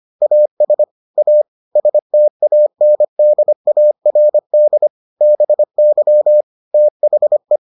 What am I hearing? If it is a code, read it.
Morse code, heard as a keyed tone, about 25 wpm, 600 Hz. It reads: AS A STANDARD BY THE